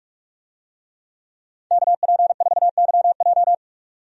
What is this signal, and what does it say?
Morse code, heard as a keyed tone, about 46 wpm, 700 Hz. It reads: KP4YJ
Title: DXCC-100 Call Signs - 5 Characters - Repeated 2x - 45wpm